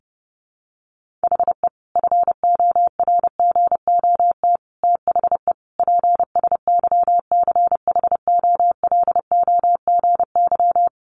Morse code, heard as a keyed tone, about 30 wpm, 700 Hz. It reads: HE FORGOT THE PSYCHOLOGY
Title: Sentences from Top 1400 Words 30wpm